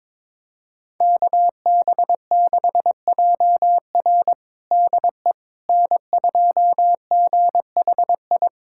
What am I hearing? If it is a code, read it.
Morse code, heard as a keyed tone, about 21 wpm, 700 Hz. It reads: KB6JR DE N2GHI
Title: USA Callsign DE Callsign 22wpm